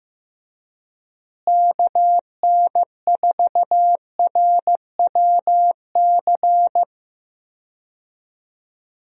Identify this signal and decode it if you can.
Morse code, heard as a keyed tone, about 15 wpm, 700 Hz. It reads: KN4RWC
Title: USA Call Signs - 6 Characters - Repeated 2x - 15wpm